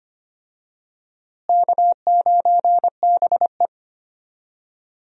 Morse code, heard as a keyed tone, about 25 wpm, 700 Hz. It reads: K9BE
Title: USA Call Signs - 4 Characters - Repeated 3x - 25wpm